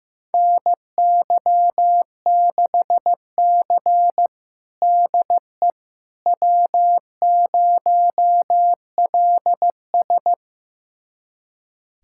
Morse code, heard as a keyed tone, about 16 wpm, 700 Hz. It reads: NY6C DE W0LS
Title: USA Callsign DE Callsign - 4 Characters - 15wpm